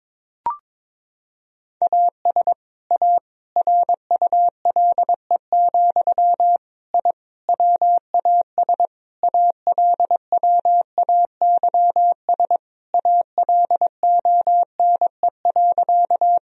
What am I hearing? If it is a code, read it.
Morse code, heard as a keyed tone, about 22 wpm, 700 Hz. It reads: AS A RULE, I WAS ALWAYS ALONE.
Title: Sentences from Top 1000 Words - Review Part 2 - 22wpm